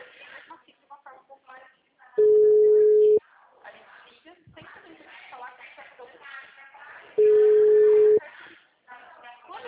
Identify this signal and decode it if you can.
Ringback tone (Europe, 425 Hz), with speech.
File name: out- -2011- - - .37989